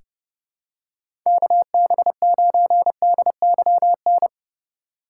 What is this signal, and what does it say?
Morse code, heard as a keyed tone, about 32 wpm, 700 Hz. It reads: KB9DYN
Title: USA Call Signs - 6 Characters - 3x Character Spacing - 30wpm